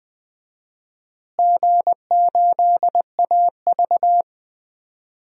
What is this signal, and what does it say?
Morse code, heard as a keyed tone, about 20 wpm, 700 Hz. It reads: G8AV
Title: DXCC-100 Call Signs - 4 Characters - Repeated 3x - 20wpm